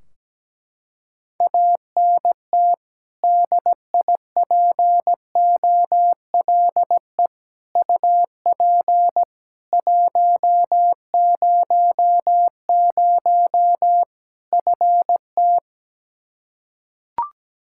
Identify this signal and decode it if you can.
Morse code, heard as a keyed tone, about 17 wpm, 700 Hz. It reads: ANT DIPOLE UP 100 FT
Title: Practice QSO - Exchange 2 - Info - 17wpm